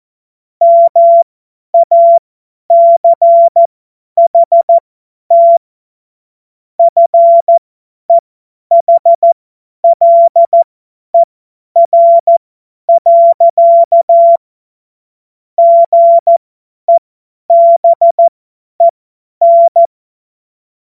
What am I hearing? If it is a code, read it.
Morse code, heard as a keyed tone, about 14 wpm, spaced as for 10 wpm, 680 Hz. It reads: MACHT FEHLER. GEBEN
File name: Morsekurs / Witze / Witz 06. Dezember 2022 50 F70